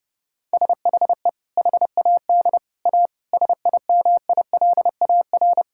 Morse code, heard as a keyed tone, about 32 wpm, 700 Hz. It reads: SHE HAD A SIMILAR
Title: Sentences from Top 1000 Words - Review Part 5 - 30wpm